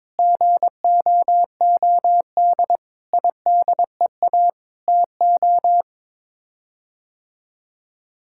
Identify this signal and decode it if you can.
Morse code, heard as a keyed tone, about 22 wpm, 700 Hz. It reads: GOOD IDEA TO
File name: practice / 22wpm / Sets of 7 Words - 22wpm